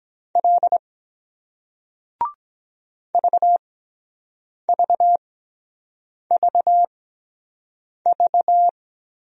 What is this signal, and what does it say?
Morse code, heard as a keyed tone, about 23 wpm, spaced as for 7 wpm, 700 Hz. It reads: L VVVV